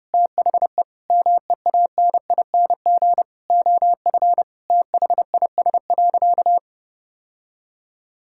Morse code, heard as a keyed tone, about 30 wpm, 700 Hz. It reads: THE MEANING OF THIS.
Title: Sentences from Top 1000 Words - Review Part 6 - 30wpm